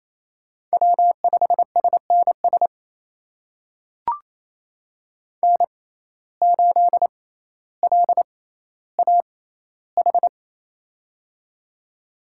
Morse code, heard as a keyed tone, about 28 wpm, spaced as for 14 wpm, 700 Hz. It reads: W5SNS N8LAH
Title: USA Call Signs - 5 Characters - 6x Character Spacing - 28wpm